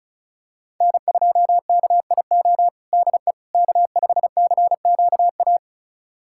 Morse code, heard as a keyed tone, about 36 wpm, 700 Hz. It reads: N2KIO DE K5CQA
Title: USA Callsign DE Callsign - 5 Characters - 35wpm